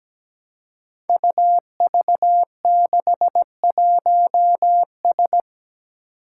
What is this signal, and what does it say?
Morse code, heard as a keyed tone, about 17 wpm, 700 Hz. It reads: UV61S
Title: Five Challenging Letter-Number Combinations 17wpm